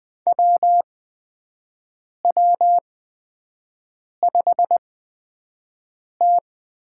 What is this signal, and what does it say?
Morse code, heard as a keyed tone, about 20 wpm, spaced as for 5 wpm, 700 Hz. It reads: WW5T
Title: USA Call Signs - 4 Characters - 8x Character Spacing - 20wpm